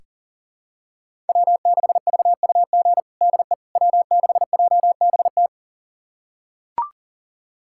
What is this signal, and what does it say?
Morse code, heard as a keyed tone, about 40 wpm, 700 Hz. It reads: W6VUG DE W6JBT
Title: USA Callsign DE Callsign - 5 Characters - 40wpm